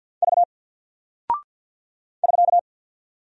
Morse code, heard as a keyed tone, about 48 wpm, 700 Hz. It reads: V <SK>
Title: Single Character (Letter:Number:Punctuation:Prosign) Alt 50wpm